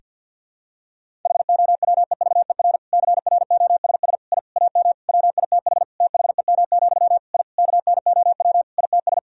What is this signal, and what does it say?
Morse code, heard as a keyed tone, about 47 wpm, 700 Hz. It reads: HOWEVER CROSS I AM WITH THEM, I KNOW ITS
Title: Sentences from Top 1000 Words - Review Part 1 - 50wpm